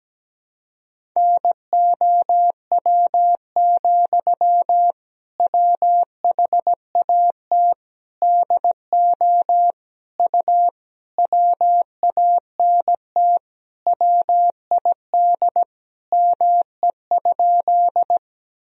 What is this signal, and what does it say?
Morse code, heard as a keyed tone, about 17 wpm, 700 Hz. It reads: NOW, WHAT DO U WANT WID ME?
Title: Sentences from Top 200 Words and Abbreviations 17wpm